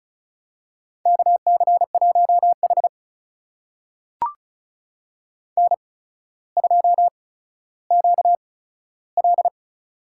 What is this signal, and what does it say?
Morse code, heard as a keyed tone, about 36 wpm, spaced as for 15 wpm, 700 Hz. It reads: KC1H N2QL
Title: USA Call Signs - 4 Characters - 8x Character Spacing - 35wpm